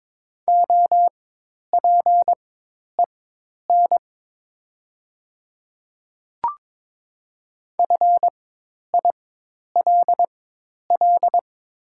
Morse code, heard as a keyed tone, about 22 wpm, spaced as for 10 wpm, 700 Hz. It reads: OPEN FILL
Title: 4x Character Spacing - Top 500 Words - 22wpm